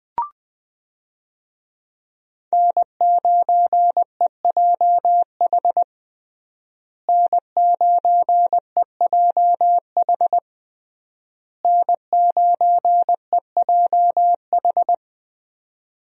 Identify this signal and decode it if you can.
Morse code, heard as a keyed tone, about 20 wpm, 700 Hz. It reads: N9EJH N9EJH N9EJH
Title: USA Call Signs - 5 Characters - Repeated 3x - 20wpm